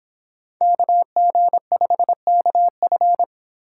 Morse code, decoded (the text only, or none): KG5KF